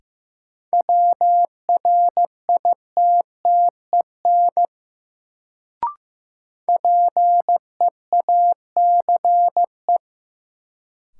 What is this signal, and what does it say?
Morse code, heard as a keyed tone, about 15 wpm, 700 Hz. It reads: WRITTEN PEACE
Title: Sentences from Top 1500 Words - New Word Drill - 15wpm